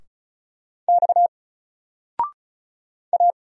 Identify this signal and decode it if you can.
Morse code, heard as a keyed tone, about 35 wpm, 700 Hz. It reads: X A
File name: Single-Letter-Number-35wpm